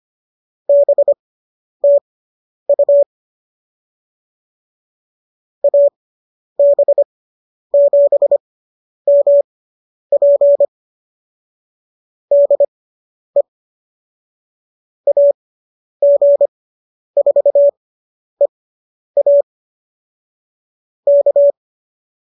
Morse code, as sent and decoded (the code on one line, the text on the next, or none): -... - ..- / .- -... --... -- .--. / -.. / .- --. ....- . .- / -.-
BTU AB7MP DE AG4EA K